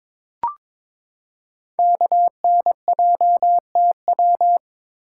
Morse code, heard as a keyed tone, about 22 wpm, 700 Hz. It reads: KNJTW
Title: Code Groups - 5 Letter:Numbers - 22wpm